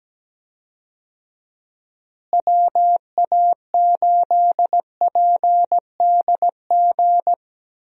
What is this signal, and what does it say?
Morse code, heard as a keyed tone, about 17 wpm, 700 Hz. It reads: WA8PDG
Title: USA Call Signs - 6 Characters - Repeated 3x - 17wpm